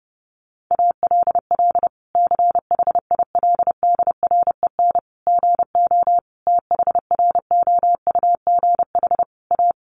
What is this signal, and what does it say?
Morse code, heard as a keyed tone, about 30 wpm, 700 Hz. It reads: ALL CHILDREN GO THROUGH A
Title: Sentences from Top 1200 Words 30wpm